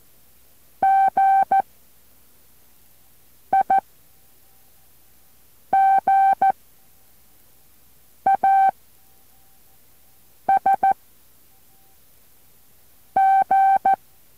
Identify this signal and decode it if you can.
Morse code, heard as a keyed tone, about 14 wpm, 780 Hz. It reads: G I G A S G